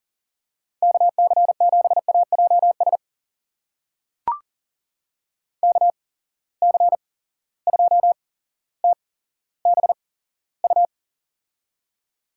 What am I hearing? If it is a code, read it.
Morse code, heard as a keyed tone, about 40 wpm, spaced as for 17 wpm, 700 Hz. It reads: KC7AJS KC2TBU
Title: USA Call Signs - 6 Characters - 8x Character Spacing - 40wpm